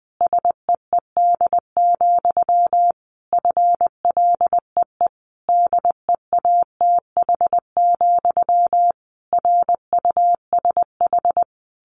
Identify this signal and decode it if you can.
Morse code, heard as a keyed tone, about 20 wpm, 700 Hz. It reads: SEED, FLEE DEATH, RUSH
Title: A Sound Of Thunder 20wpm